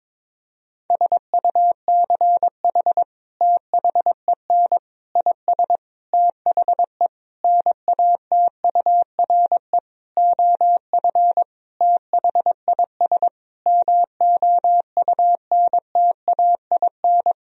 Morse code, heard as a keyed tone, about 22 wpm, 700 Hz. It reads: SUCH THEN IS THE NATURE OF THIS MOUNTAIN
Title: Sentences from Top 1000 Words - Review Part 4 - 22wpm